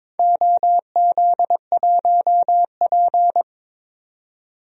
Morse code, heard as a keyed tone, about 21 wpm, 700 Hz. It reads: OZ1P
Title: DXCC-100 Call Signs - 4 Characters - 22wpm